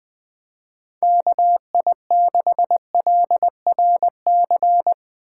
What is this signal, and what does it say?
Morse code, heard as a keyed tone, about 20 wpm, 700 Hz. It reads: KI6LRC
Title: Speed Racing - USA Call Signs 17wpm